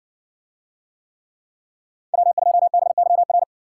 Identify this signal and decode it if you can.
Morse code, heard as a keyed tone, about 67 wpm, 700 Hz. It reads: W2BYR